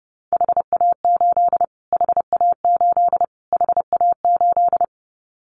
Morse code, heard as a keyed tone, about 30 wpm, 700 Hz. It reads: HA8 HA8 HA8
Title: Callsign Prefixes for Top 25 DXCC - Repeated 3x - 30wpm